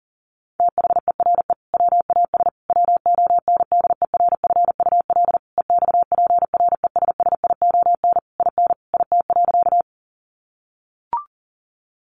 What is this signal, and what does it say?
Morse code, heard as a keyed tone, about 40 wpm, 700 Hz. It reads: THERE WAS WONDERFUL EXPRESSION IN IT.
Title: Sentences from Top 1200 Words 40wpm